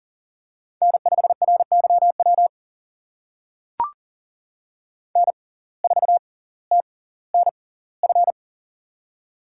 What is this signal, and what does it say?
Morse code, heard as a keyed tone, about 41 wpm, spaced as for 21 wpm, 690 Hz. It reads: N5RYW N4TNF
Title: USA Call Signs - 5 Characters - 6x Character Spacing - 40wpm